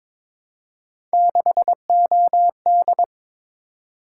Morse code, heard as a keyed tone, about 22 wpm, 700 Hz. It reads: 6OD
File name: practice / 22wpm / Three Letter-Number Combinations - 22wpm